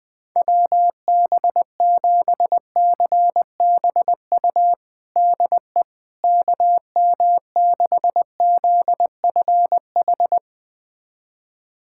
Morse code, heard as a keyed tone, about 21 wpm, 700 Hz. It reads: WB7CBU DE KM6ZFH
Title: USA Callsign DE Callsign - 6 Characters - 20wpm